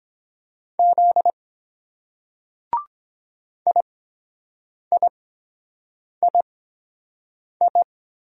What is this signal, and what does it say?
Morse code, heard as a keyed tone, about 23 wpm, spaced as for 7 wpm, 700 Hz. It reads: Z IIII